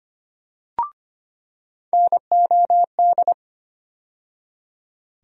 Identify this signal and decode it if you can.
Morse code, heard as a keyed tone, about 25 wpm, 700 Hz. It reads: NOD
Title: USA Call Sign Postfixes - 3 Letters 25wpm